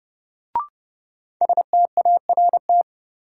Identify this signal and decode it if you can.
Morse code, heard as a keyed tone, about 30 wpm, 700 Hz. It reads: START
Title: Sentences from Top 300 Words - New Word Drill - 30wpm